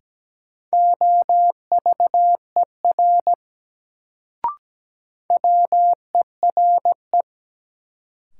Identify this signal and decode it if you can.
Morse code, heard as a keyed tone, about 17 wpm, 700 Hz. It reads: OVER WERE
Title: Sentences from Top 100 Words - New Word Drill - 17wpm